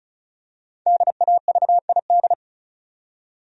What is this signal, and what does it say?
Morse code, heard as a keyed tone, about 36 wpm, 700 Hz. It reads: DAVID
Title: USA Common Names - Max 5 Letters - 35wpm